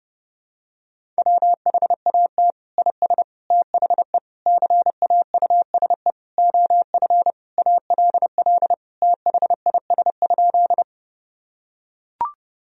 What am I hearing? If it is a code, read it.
Morse code, heard as a keyed tone, about 30 wpm, 700 Hz. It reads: WHAT IS THE CAUSE OF ALL THIS?
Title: Sentences from Top 1000 Words - Review Part 1 - 30wpm